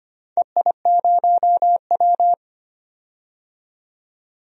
Morse code, heard as a keyed tone, about 25 wpm, 700 Hz. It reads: EI0W
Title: DXCC-100 Call Signs - 4 Characters - 25wpm